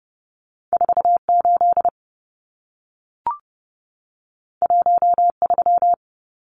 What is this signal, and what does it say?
Morse code, heard as a keyed tone, about 30 wpm, 700 Hz. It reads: 48 13
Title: Numbers - 2-Digit 30wpm